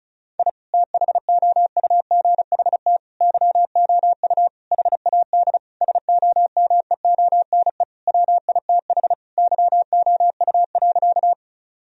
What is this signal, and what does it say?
Morse code, heard as a keyed tone, about 36 wpm, 700 Hz. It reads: I THOUGHT YOU HAD SOMEONE WITH YOU.
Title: Sentences from Top 1000 Words - Review Part 4 - 35wpm